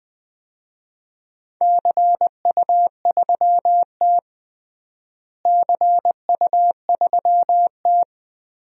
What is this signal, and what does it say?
Morse code, heard as a keyed tone, about 20 wpm, 700 Hz. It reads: CU3T CU3T
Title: DXCC-100 Call Signs - 4 Characters - Repeated 3x - 20wpm